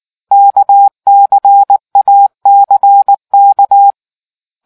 Morse code, heard as a keyed tone, about 19 wpm, 800 Hz. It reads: KCACK